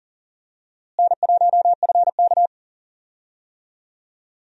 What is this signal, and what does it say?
Morse code, heard as a keyed tone, about 40 wpm, 700 Hz. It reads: N1FK
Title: USA Call Signs - 4 Characters - Repeated 3x - 40wpm